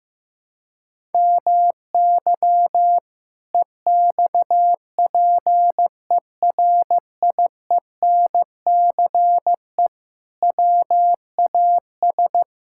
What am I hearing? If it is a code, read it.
Morse code, heard as a keyed tone, about 16 wpm, 700 Hz. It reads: MY EXPERIENCE WAS
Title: Sentences from Top 1000 Words - Review Part 6 - 15wpm